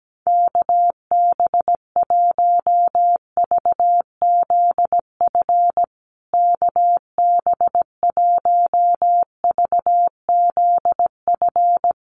Morse code, heard as a keyed tone, about 17 wpm, 700 Hz. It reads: KB1VZF KB1VZF
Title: USA Call Signs Repeated 2x 17wpm